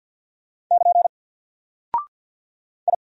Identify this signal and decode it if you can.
Morse code, heard as a keyed tone, about 50 wpm, 700 Hz. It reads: <KN> I
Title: Single Character (Letter:Number:Punctuation:Prosign) 50wpm